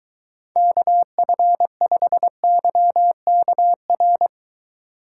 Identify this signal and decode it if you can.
Morse code, heard as a keyed tone, about 24 wpm, 700 Hz. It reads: KF5YKR